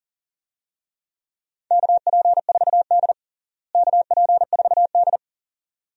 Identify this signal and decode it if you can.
Morse code, heard as a keyed tone, about 37 wpm, 700 Hz. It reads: KP4D KP4D
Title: DXCC-100 Call Signs - 4 Characters - Repeated 2x - 40wpm